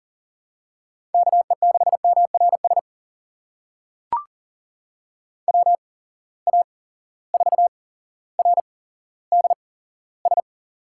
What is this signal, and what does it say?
Morse code, heard as a keyed tone, about 38 wpm, spaced as for 17 wpm, 700 Hz. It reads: KE6MRS WA4RDS
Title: USA Call Signs - 6 Characters - 8x Character Spacing - 40wpm